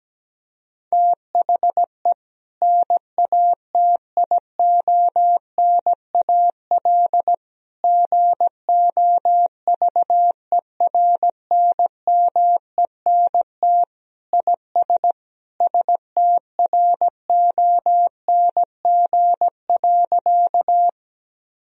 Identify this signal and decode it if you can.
Morse code, heard as a keyed tone, about 17 wpm, 700 Hz. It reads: THE NATIONAL GOVERNMENT IS STRONG.
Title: Sentences from Top 1000 Words - Review Part 4 - 17wpm